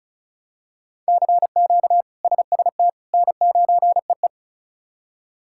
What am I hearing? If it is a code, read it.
Morse code, heard as a keyed tone, about 36 wpm, 700 Hz. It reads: CQ SST N9EE
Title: K1USN SST QSO Practice - 35wpm